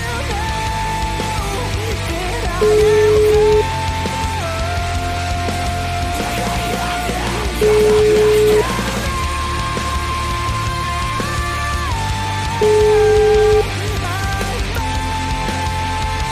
Ringback tone (Europe, 425 Hz), with noise.